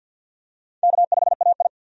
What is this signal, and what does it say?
Morse code, heard as a keyed tone, about 52 wpm, 700 Hz. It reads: K5AI
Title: USA Call Signs - 4 Characters - 50wpm